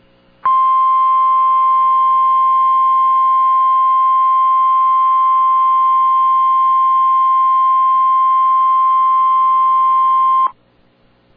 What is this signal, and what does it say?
Emergency Alert System (SAME) tones (the 1050 Hz attention tone). It is heard from 0.4 to 10.5 s.